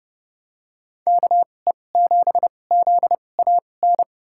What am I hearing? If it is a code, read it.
Morse code, heard as a keyed tone, about 30 wpm, spaced as for 22 wpm, 700 Hz. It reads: KE7ZAN
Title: USA Call Signs - 6 Characters - 2x Character Spacing - 30wpm